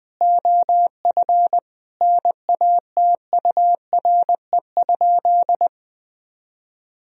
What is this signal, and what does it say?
Morse code, heard as a keyed tone, about 20 wpm, 700 Hz. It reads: OF NATURE?
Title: Sentences from Top 1000 Words - Review Part 3 - 20wpm